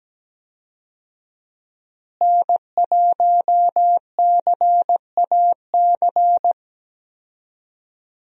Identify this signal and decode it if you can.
Morse code, heard as a keyed tone, about 17 wpm, 700 Hz. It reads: N1CAC
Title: USA Call Signs - 5 Characters - Repeated 2x - 17wpm